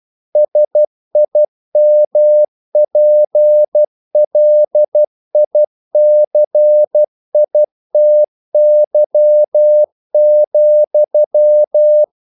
Morse code, heard as a keyed tone, about 12 wpm, 600 Hz. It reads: SIMPLICITY,
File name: mp3 files / the gardener - 12wpm